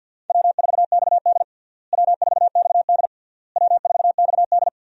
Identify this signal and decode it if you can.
Morse code, heard as a keyed tone, about 52 wpm, 700 Hz. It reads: W4XD W4XD W4XD
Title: USA Call Signs - 4 Characters - Repeated 3x - 50wpm